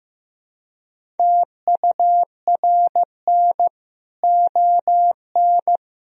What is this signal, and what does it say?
Morse code, heard as a keyed tone, about 16 wpm, 700 Hz. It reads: TURN ON
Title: Sentences from Top 1000 Words - Review Part 3 - 15wpm